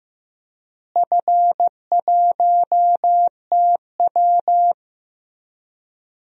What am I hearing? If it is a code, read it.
Morse code, heard as a keyed tone, about 16 wpm, 700 Hz. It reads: F1TW
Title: DXCC-100 Call Signs - 4 Characters - Repeated 3x - 15wpm